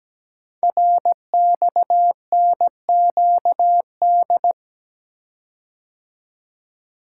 Morse code, heard as a keyed tone, about 17 wpm, 700 Hz. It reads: RXNQD
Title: Code Groups - 5 Letters - 17wpm